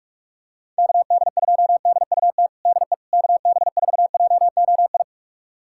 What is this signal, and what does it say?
Morse code, heard as a keyed tone, about 46 wpm, 700 Hz. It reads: KD2DUT DE KB4JOI
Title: USA Callsign DE Callsign - 6 Characters - 45wpm